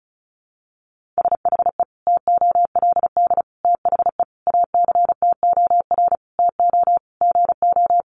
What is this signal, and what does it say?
Morse code, heard as a keyed tone, about 34 wpm, 700 Hz. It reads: SHE TOLD THE ACTOR TO GO
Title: Sentences from Top 1500 Words 35wpm